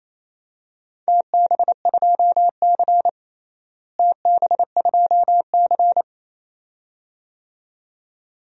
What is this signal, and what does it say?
Morse code, heard as a keyed tone, about 28 wpm, 700 Hz. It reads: TB2C TB2C
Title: DXCC-100 Call Signs - 4 Characters - Repeated 2x - 28wpm